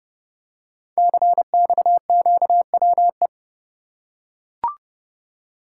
Morse code, heard as a keyed tone, about 30 wpm, 700 Hz. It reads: CXQWE
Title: Code Groups - 5 Letters - 30wpm